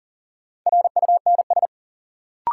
Morse code, heard as a keyed tone, about 40 wpm, 700 Hz. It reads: RUNS
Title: Sentences from Top 600 Words - New Word Drill - 40wpm